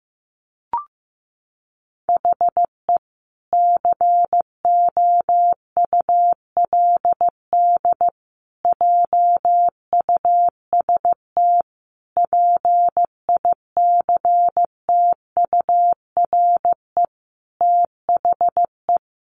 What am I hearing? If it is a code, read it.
Morse code, heard as a keyed tone, about 15 wpm, 700 Hz. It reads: HE COULD JUST PICTURE THE
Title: Sentences from Top 1100 Words 15wpm